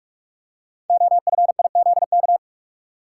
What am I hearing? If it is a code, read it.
Morse code, heard as a keyed tone, about 44 wpm, 690 Hz. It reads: OFIZK
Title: Code Groups - 5 Letters - 45wpm